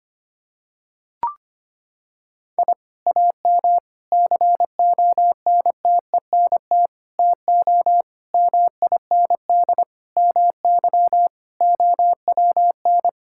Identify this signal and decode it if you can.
Morse code, heard as a keyed tone, about 25 wpm, 700 Hz. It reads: I AM CONTENT TO MIND MY OWN
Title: Sentences from Top 1000 Words - Review Part 6 - 25wpm